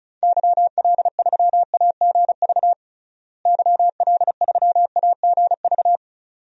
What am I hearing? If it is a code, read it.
Morse code, heard as a keyed tone, about 36 wpm, 700 Hz. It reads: YL3AGV YL3AGV
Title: DXCC-100 Call Signs - 6 Characters - Repeated 2x - 35wpm